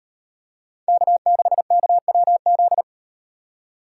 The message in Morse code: K6KWZ

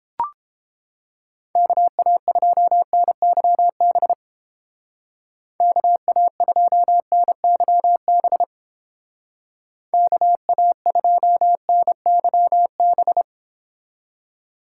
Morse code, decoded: KA2NYB KA2NYB KA2NYB

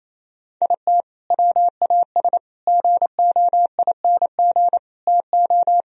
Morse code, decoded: IT WAS GOING TO

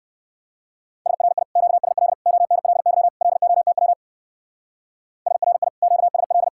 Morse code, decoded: SRI QSL ONLY LOTW SRI QSL

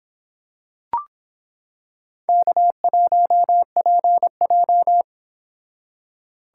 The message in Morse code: K1PJ

